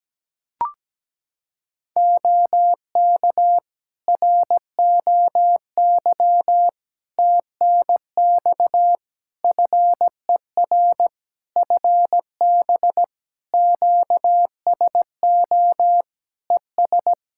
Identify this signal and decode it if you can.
Morse code, heard as a keyed tone, about 17 wpm, 700 Hz. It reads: OK ROY TNX FER FB QSO ES
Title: Practice QSO - Exchange 4 - Ending - 17wpm